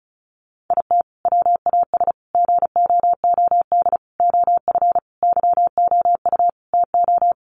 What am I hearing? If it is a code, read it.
Morse code, heard as a keyed tone, about 34 wpm, 700 Hz. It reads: IT WAS GOOD OF YOU TO
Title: Sentences from Top 1100 Words 35wpm